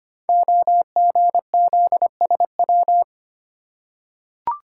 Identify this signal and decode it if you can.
Morse code, heard as a keyed tone, about 25 wpm, 700 Hz. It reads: OGZSW
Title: Code Groups - 5 Letter:Numbers - 25wpm